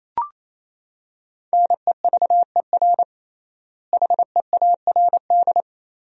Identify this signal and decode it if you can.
Morse code, heard as a keyed tone, about 28 wpm, 700 Hz. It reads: NEVER HEARD
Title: 3x Word-Spacing - Sentences from Top 400 Words - 28wpm